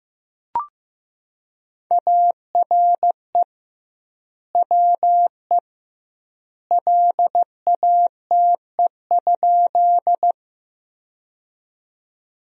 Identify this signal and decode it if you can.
Morse code, heard as a keyed tone, about 15 wpm, 700 Hz. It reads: ARE WE LATE?
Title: 2x Word-Spacing - Sentences from Top 300 Words - 15wpm